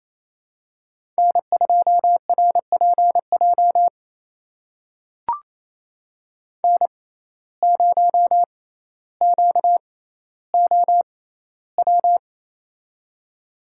Morse code, heard as a keyed tone, about 28 wpm, spaced as for 14 wpm, 700 Hz. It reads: N2RPJ N0QOW
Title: USA Call Signs - 5 Characters - 6x Character Spacing - 28wpm